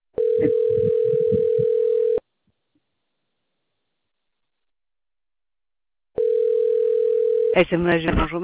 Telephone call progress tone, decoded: ringback tone